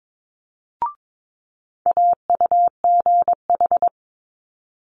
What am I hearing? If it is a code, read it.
Morse code, heard as a keyed tone, about 22 wpm, 700 Hz. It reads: AUGH
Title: Multi-letter Phonemes Quick Fire 22wpm